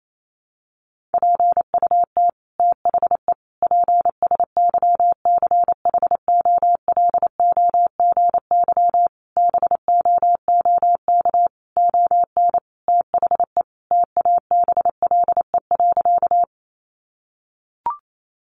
Morse code, heard as a keyed tone, about 28 wpm, 700 Hz. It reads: PUT THE PSYCHOLOGY BOOK ON THE TABLE.
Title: Sentences from Top 1400 Words 28wpm